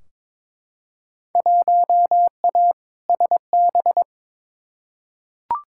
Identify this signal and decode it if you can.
Morse code, heard as a keyed tone, about 22 wpm, 700 Hz. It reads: 1A SB
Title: Field Day Exchanges Encore - 22wpm